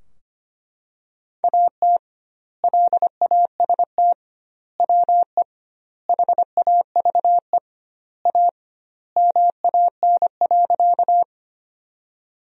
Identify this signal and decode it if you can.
Morse code, heard as a keyed tone, about 25 wpm, 700 Hz. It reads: AT LAST WE HAVE A MAN.
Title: 2x Word-Spacing - Sentences from Top 300 Words - 25wpm